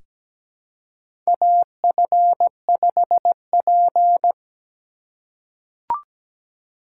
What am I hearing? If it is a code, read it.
Morse code, heard as a keyed tone, about 17 wpm, 700 Hz. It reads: AF5P
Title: K1USN SST QSO Practice - 17wpm